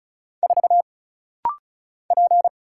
Morse code, heard as a keyed tone, about 34 wpm, 700 Hz. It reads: 4 P
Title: Single Letter-Number - Rapid-Fire 35wpm